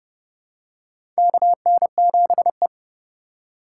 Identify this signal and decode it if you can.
Morse code, heard as a keyed tone, about 30 wpm, 700 Hz. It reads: KN7E
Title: USA Call Signs - 4x Character Spacing - 30wpm